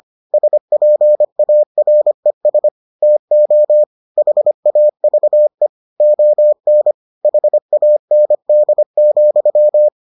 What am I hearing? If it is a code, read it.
Morse code, heard as a keyed tone, about 25 wpm, 600 Hz. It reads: SPARES TO HAVE ON HAND,